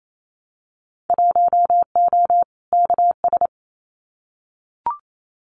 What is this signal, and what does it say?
Morse code, heard as a keyed tone, about 28 wpm, 700 Hz. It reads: 1O KS
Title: Winter Field Day Exchanges 28wpm